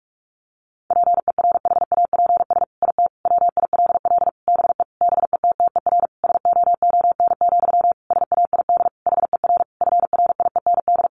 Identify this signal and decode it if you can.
Morse code, heard as a keyed tone, about 44 wpm, 700 Hz. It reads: PERHAPS IT WILL BE BETTER SOON, SAID HER FRIEND
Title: Sentences from Top 1500 Words 45wpm